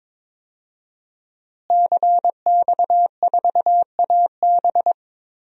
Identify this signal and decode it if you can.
Morse code, heard as a keyed tone, about 22 wpm, 700 Hz. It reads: CX4AB
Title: DXCC-100 Call Signs - 5 Characters - Repeated 3x - 22wpm